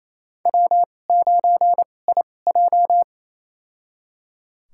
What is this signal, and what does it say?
Morse code, heard as a keyed tone, about 27 wpm, spaced as for 20 wpm, 700 Hz. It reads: W9IJ
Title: USA Call Signs - 2x Character Spacing - 28wpm